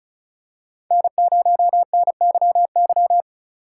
Morse code, heard as a keyed tone, about 34 wpm, 700 Hz. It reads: N0NYY
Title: USA Call Signs - 5 Characters - 2x Character Spacing - 35wpm